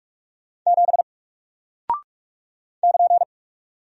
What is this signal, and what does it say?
Morse code, heard as a keyed tone, about 44 wpm, 700 Hz. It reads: 7 <KN>
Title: Single Character (Letter:Number:Punctuation:Prosign) Alt 45wpm